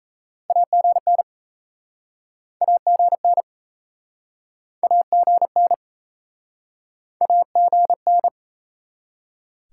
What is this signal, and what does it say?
Morse code, heard as a keyed tone, about 39 wpm, 700 Hz. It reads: AGN AGN AGN AGN